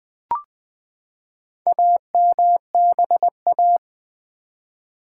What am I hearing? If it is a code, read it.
Morse code, heard as a keyed tone, about 20 wpm, 700 Hz. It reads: AMBA
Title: Build A Word - 4 Letters At A Time - 10-Letter Words - 20wpm